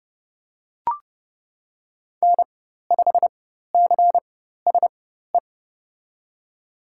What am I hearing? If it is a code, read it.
Morse code, heard as a keyed tone, about 30 wpm, spaced as for 14 wpm, 700 Hz. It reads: N5CSE
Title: USA Call Signs - 5 Characters - 4x Character Spacing - 30wpm